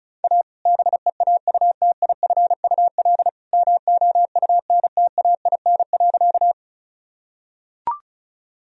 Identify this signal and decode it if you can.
Morse code, heard as a keyed tone, about 34 wpm, 700 Hz. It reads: A BEAUTIFUL MOUNTAIN.
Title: Sentences from Top 1000 Words - Review Part 1 - 35wpm